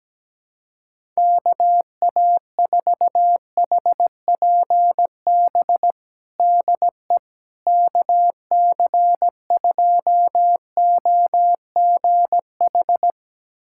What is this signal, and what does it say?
Morse code, heard as a keyed tone, about 17 wpm, 700 Hz. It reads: KA4HPB DE KC2OGH
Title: USA Callsign DE Callsign 17wpm